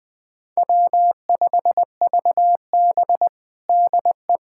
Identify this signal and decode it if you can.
Morse code, heard as a keyed tone, about 21 wpm, 700 Hz. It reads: W5VB DE
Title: USA Callsign DE Callsign - 4 Characters - 20wpm